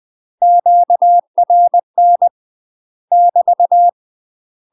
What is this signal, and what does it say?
Morse code, heard as a keyed tone, about 20 wpm, 700 Hz. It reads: QRN <BT>
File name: UK MASTER QSOs TRAINING / 20WPM / UK MASTER QSOs 0023